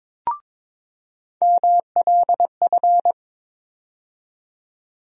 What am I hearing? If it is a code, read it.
Morse code, heard as a keyed tone, about 22 wpm, 700 Hz. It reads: MLF
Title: USA Call Sign Postfixes - 3 Letters 22wpm